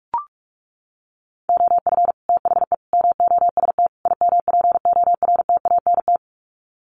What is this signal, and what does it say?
Morse code, heard as a keyed tone, about 44 wpm, 700 Hz. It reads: OF THE MOST IMPORTANT
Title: Sets of 4 Words 45wpm